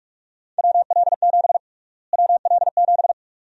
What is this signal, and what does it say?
Morse code, heard as a keyed tone, about 45 wpm, 700 Hz. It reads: WL7 WL7
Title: USA Call Sign Prefixes - Repeated 2x - 45wpm